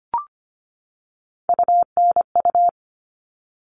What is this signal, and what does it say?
Morse code, heard as a keyed tone, about 25 wpm, 700 Hz. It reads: UNU
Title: All Cut Numbers - 3-Digit 25wpm